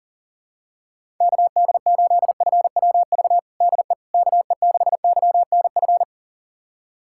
Morse code, heard as a keyed tone, about 40 wpm, 700 Hz. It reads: KD8FWV DE KE6YNF
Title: USA Callsign DE Callsign 40wpm